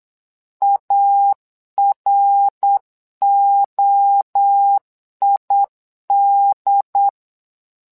Morse code, heard as a keyed tone, about 8 wpm, 800 Hz. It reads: AROID
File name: LEZIONI / 3 MWRO / 3 MWRO 8wpm 2606